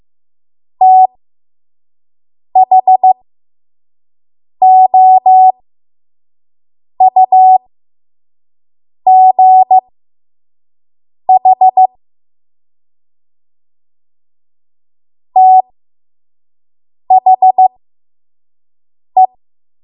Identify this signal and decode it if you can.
Morse code, heard as a keyed tone, about 15 wpm, spaced as for 5 wpm, 750 Hz. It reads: THOUGH THE
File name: Archive / 5 WPM / 05WPM